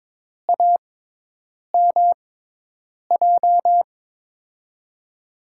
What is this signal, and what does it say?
Morse code, heard as a keyed tone, about 22 wpm, spaced as for 8 wpm, 700 Hz. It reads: AMJ
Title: USA Call Signs - 6 Characters - 6x Character Spacing - 22wpm